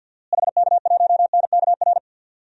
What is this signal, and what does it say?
Morse code, heard as a keyed tone, about 47 wpm, 700 Hz. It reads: HK1NKR
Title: DXCC-100 Call Signs - 6 Characters - Repeated 2x - 50wpm